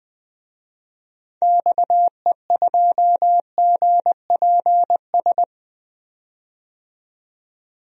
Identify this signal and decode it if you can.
Morse code, heard as a keyed tone, about 20 wpm, 700 Hz. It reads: XE2GPS
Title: DXCC-100 Call Signs - 6 Characters - 20wpm